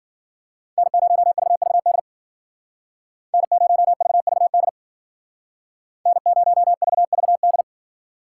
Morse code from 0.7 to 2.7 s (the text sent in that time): N0VVD